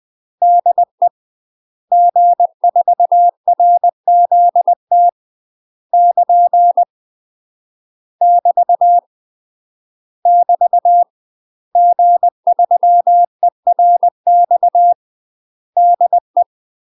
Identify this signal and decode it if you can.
Morse code, heard as a keyed tone, about 20 wpm, 700 Hz. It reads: DE G4RZT <KN> <BT> <BT> G3ERX DE
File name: UK MASTER QSOs TRAINING / 20WPM / UK MASTER QSOs 0068